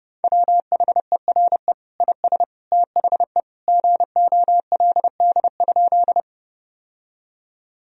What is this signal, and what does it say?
Morse code, heard as a keyed tone, about 30 wpm, 700 Hz. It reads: WHERE IS THE GOLD?
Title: Sentences from Top 1000 Words - Review Part 3 - 30wpm